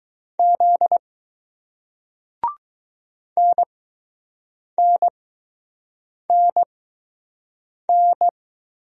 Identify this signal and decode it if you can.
Morse code, heard as a keyed tone, about 20 wpm, spaced as for 6 wpm, 700 Hz. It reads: Z NNNN